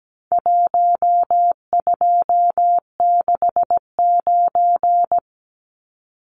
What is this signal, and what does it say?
Morse code, heard as a keyed tone, about 17 wpm, 700 Hz. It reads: 1269
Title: Numbers - Contest Style - 4-Digit 17wpm